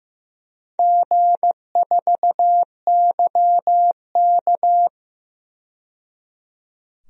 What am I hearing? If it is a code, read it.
Morse code, heard as a keyed tone, about 15 wpm, 700 Hz. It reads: G4YK
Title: DXCC-100 Call Signs - 4 Characters - 15wpm